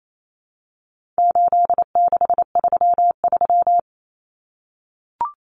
Morse code, heard as a keyed tone, about 28 wpm, 700 Hz. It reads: 8633
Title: Numbers - 4-Digit 28wpm